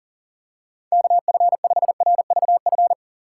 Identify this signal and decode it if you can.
Morse code, heard as a keyed tone, about 40 wpm, 700 Hz. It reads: KF5RVF
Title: USA Call Signs - 6 Characters - 3x Character Spacing - 40wpm